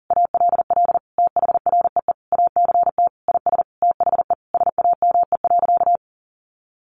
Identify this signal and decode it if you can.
Morse code, heard as a keyed tone, about 37 wpm, 700 Hz. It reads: ALL THREE ACT IS THE SAME.
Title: Sentences from Top 1400 Words 40wpm